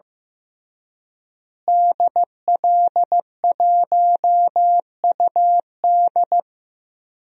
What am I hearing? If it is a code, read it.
Morse code, heard as a keyed tone, about 15 wpm, 700 Hz. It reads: DL1UD
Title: DXCC-100 Call Signs - 5 Characters - Repeated 3x - 15wpm